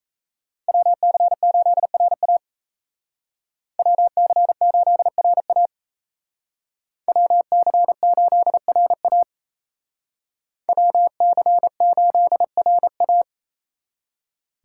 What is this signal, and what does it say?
Morse code, heard as a keyed tone, about 37 wpm, 700 Hz. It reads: WC8RA WC8RA WC8RA WC8RA